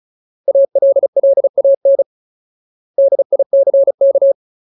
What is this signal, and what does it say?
Morse code, heard as a keyed tone, about 34 wpm, 550 Hz. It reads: ALLAN DICK
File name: Ultimate Word List / 35WPM Ultimate